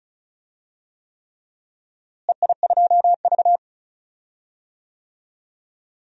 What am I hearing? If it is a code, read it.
Morse code, heard as a keyed tone, about 36 wpm, 700 Hz. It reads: EI2V
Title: DXCC-100 Call Signs - 4 Characters - Repeated 3x - 35wpm